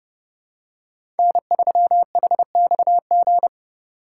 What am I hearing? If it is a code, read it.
Morse code, heard as a keyed tone, about 30 wpm, 700 Hz. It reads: N3HXG
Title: USA Call Signs - 5 Characters - 4x Character Spacing - 30wpm